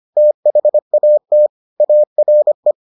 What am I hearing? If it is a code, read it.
Morse code, heard as a keyed tone, about 25 wpm, 600 Hz. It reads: THAT ARE